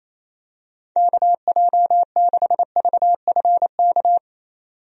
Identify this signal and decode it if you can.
Morse code, heard as a keyed tone, about 28 wpm, 700 Hz. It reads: KJ6VFK
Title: USA Call Signs - 6 Characters - 28wpm